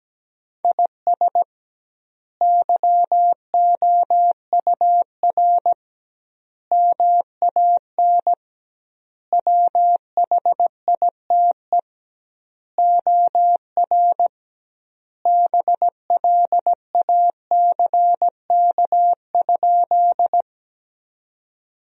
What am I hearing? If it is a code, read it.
Morse code, heard as a keyed tone, about 17 wpm, 700 Hz. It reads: IS YOUR MAN WHITE OR BLACK?
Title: 2x Word-Spacing - Sentences from Top 400 Words - 17wpm